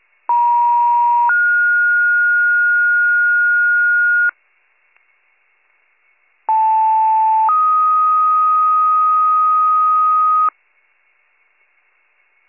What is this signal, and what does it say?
Two-tone page, heard as a one-second tone, then a three-second tone, about 950 then 1430 Hz, about 880 then 1250 Hz. It is heard from 0.3 to 10.5 s.